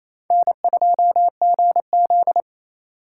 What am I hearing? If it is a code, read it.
Morse code, heard as a keyed tone, about 29 wpm, 700 Hz. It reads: N2GZ
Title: USA Call Signs - 4 Characters - 3x Character Spacing - 28wpm